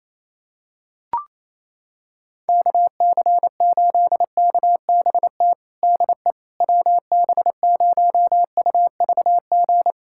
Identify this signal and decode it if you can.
Morse code, heard as a keyed tone, about 28 wpm, 700 Hz. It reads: KC8KBT DE WB0UVG
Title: USA Callsign DE Callsign - 6 Characters - 28wpm